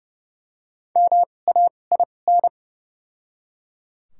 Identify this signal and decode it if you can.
Morse code, heard as a keyed tone, about 30 wpm, spaced as for 22 wpm, 700 Hz. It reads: MAIN
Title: 2x Character Spacing - Top 500 Words - 30wpm